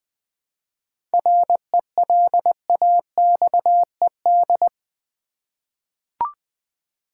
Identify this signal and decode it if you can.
Morse code, heard as a keyed tone, about 20 wpm, 700 Hz. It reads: RELAXED
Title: 7 Letter Words - First 3 Letters Given - 20wpm